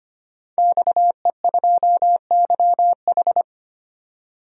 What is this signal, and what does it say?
Morse code, heard as a keyed tone, about 25 wpm, 700 Hz. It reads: XE2YH
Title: DXCC-100 Call Signs - 5 Characters - 25wpm